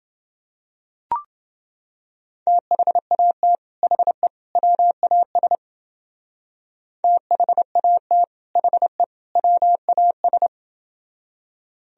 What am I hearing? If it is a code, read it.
Morse code, heard as a keyed tone, about 28 wpm, 700 Hz. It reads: THAT HE WAS THAT HE WAS